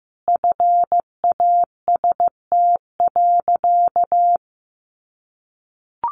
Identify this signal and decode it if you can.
Morse code, heard as a keyed tone, about 15 wpm, 700 Hz. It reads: FAST.
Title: Sentences from Top 800 Words 15wpm